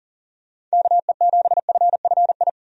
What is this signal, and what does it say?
Morse code, heard as a keyed tone, about 40 wpm, 700 Hz. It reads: KE7FFI
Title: USA Call Signs - 6 Characters - 40wpm